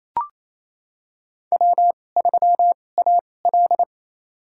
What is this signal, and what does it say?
Morse code, heard as a keyed tone, about 28 wpm, spaced as for 20 wpm, 700 Hz. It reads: W3AL
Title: USA Call Signs - 4 Characters - 2x Character Spacing - 28wpm